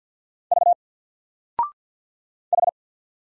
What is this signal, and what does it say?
Morse code, heard as a keyed tone, about 49 wpm, 700 Hz. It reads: V H
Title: Single Letter-Number 50wpm